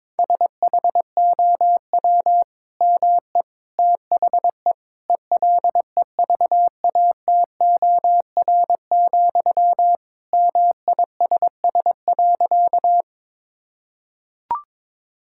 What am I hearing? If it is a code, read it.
Morse code, heard as a keyed tone, about 22 wpm, 700 Hz. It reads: SHOW ME THE ELEVATOR, MISS.
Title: Sentences from Top 1600 Words 22wpm